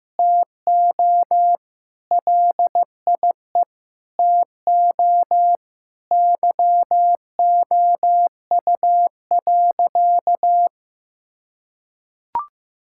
Morse code, heard as a keyed tone, about 15 wpm, 700 Hz. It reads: TO LIE TO YOU.
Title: Sentences from Top 1600 Words 15wpm